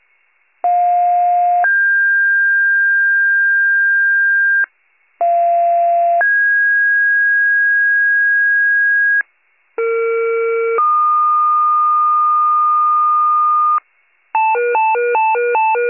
Two-tone page, heard as a one-second tone, then a three-second tone, about 710 then 1600 Hz, about 690 then 1650 Hz, about 460 then 1220 Hz, with noise. It is heard from 0.6 to 13.8 s.
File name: pager / 41 RESCUE DUTY / wcalert